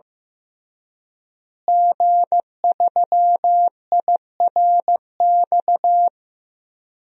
Morse code, heard as a keyed tone, about 16 wpm, 700 Hz. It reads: G3IRX
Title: DXCC-100 Call Signs - 5 Characters - Repeated 3x - 15wpm